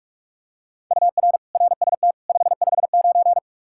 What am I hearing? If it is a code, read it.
Morse code, heard as a keyed tone, about 41 wpm, 700 Hz. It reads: UR RST 559
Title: Practice QSO - Exchange 1 - Beginning - 45wpm